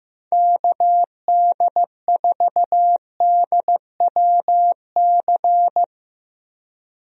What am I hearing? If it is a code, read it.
Morse code, heard as a keyed tone, about 15 wpm, 700 Hz. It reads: KD4DWC
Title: USA Call Signs - 6 Characters - Repeated 3x - 15wpm